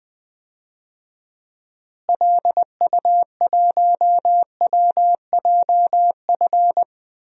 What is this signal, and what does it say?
Morse code, heard as a keyed tone, about 21 wpm, 700 Hz. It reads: LU1WJF
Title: DXCC-100 Call Signs - 6 Characters - Repeated 2x - 20wpm